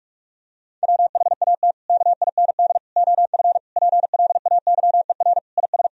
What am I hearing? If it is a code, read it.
Morse code, heard as a keyed tone, about 41 wpm, 700 Hz. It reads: WHAT KIND OF PLAYER IS
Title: Sentences from Top 1000 Words - Review Part 4 - 45wpm